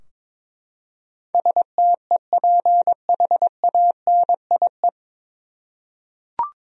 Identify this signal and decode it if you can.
Morse code, heard as a keyed tone, about 22 wpm, 700 Hz. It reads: STEPHANIE
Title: USA Common Names - 22wpm